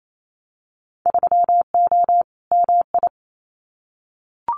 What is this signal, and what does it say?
Morse code, heard as a keyed tone, about 28 wpm, 700 Hz. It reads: 3O MI
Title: Winter Field Day Exchanges 28wpm